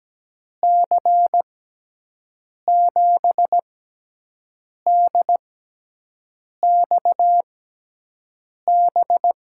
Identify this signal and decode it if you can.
Morse code, heard as a keyed tone, about 17 wpm, spaced as for 6 wpm, 700 Hz. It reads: C7DXB
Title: USA Call Signs - 6 Characters - 6x Character Spacing - 17wpm